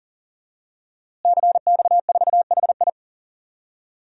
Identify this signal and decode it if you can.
Morse code, heard as a keyed tone, about 40 wpm, 700 Hz. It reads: CX4HI
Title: DXCC-100 Call Signs - 5 Characters - Repeated 2x - 40wpm